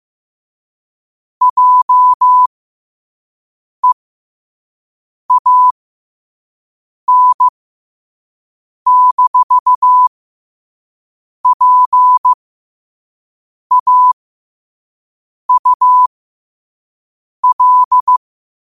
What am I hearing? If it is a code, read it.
Morse code, heard as a keyed tone, about 15 wpm, spaced as for 5 wpm, 1000 Hz. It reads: JEAN-PAUL
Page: New quotes every day in morse code at 5 Words per minute.